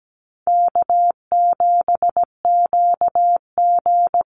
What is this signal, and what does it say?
Morse code, heard as a keyed tone, about 17 wpm, 700 Hz. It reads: K7QG
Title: USA Call Signs Repeated 3x 17wpm